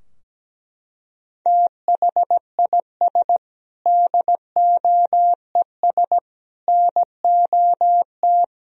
Morse code, heard as a keyed tone, about 17 wpm, 700 Hz. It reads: THIS DOES NOT
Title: Sentences from Top 1000 Words - Review Part 4 - 17wpm